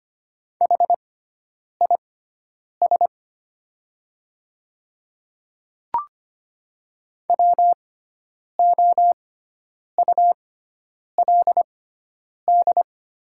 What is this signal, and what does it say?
Morse code, heard as a keyed tone, about 25 wpm, spaced as for 9 wpm, 690 Hz. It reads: HIS WOULD
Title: 6x Character Spacing - Top 300 Words - 25wpm